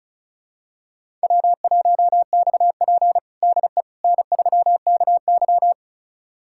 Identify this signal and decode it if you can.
Morse code, heard as a keyed tone, about 36 wpm, 700 Hz. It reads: W1XP DE N3KY
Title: USA Callsign DE Callsign - 4 Characters - 35wpm